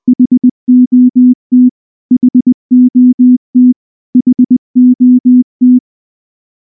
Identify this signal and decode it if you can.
Morse code, heard as a keyed tone, about 19 wpm, 250 Hz. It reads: HOT HOT HOT
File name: morse-training-20WPM